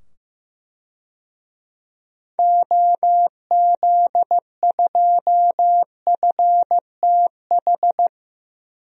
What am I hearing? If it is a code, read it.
Morse code, heard as a keyed tone, about 15 wpm, 700 Hz. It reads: OZ2FTH
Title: DXCC-100 Call Signs - 6 Characters - 15wpm